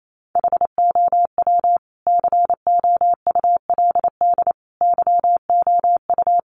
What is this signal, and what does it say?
Morse code, heard as a keyed tone, about 28 wpm, 700 Hz. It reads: HOW COULD YOU
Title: Sentences from Top 1300 Words 28wpm